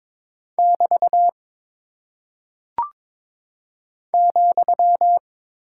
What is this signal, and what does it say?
Morse code, heard as a keyed tone, about 22 wpm, 700 Hz. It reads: <BT> ,
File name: practice / 22wpm / Single Punctuation-Prosign - 22wpm